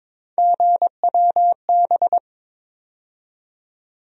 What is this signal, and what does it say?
Morse code, heard as a keyed tone, about 22 wpm, 700 Hz. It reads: GWB
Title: USA Call Sign Postfixes - 3 Letters 22wpm